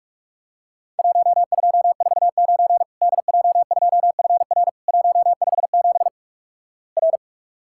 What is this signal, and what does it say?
Morse code, heard as a keyed tone, about 46 wpm, 700 Hz. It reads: 1249 DJ2FR 157 R
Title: Worked All Europe DX Contest - QTC Exchange - 45wpm